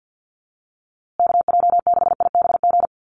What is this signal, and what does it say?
Morse code, heard as a keyed tone, about 50 wpm, 700 Hz. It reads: KJ6IBG
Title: USA Call Signs Repeated 2x 50wpm